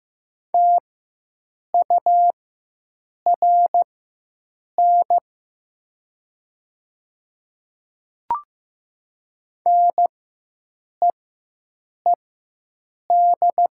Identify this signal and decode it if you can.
Morse code, heard as a keyed tone, about 15 wpm, spaced as for 7 wpm, 700 Hz. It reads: TURN NEED
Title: 4x Character Spacing - Top 200 Words - 15wpm